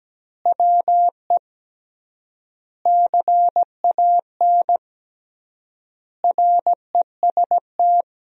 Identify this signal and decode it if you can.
Morse code, heard as a keyed tone, about 17 wpm, 700 Hz. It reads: WE CAN REST
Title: 3x Word-Spacing - Sentences from Top 500 Words - 17wpm